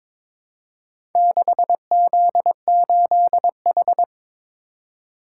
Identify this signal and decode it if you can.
Morse code, heard as a keyed tone, about 21 wpm, 700 Hz. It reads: 6Z8H
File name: practice / 22wpm / Four Letter-Number Combinations - 22wpm